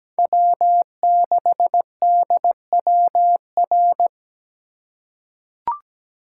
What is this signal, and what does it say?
Morse code, heard as a keyed tone, about 17 wpm, 700 Hz. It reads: W6DWR
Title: USA Call Signs - 5 Characters - 4x Character Spacing - 17wpm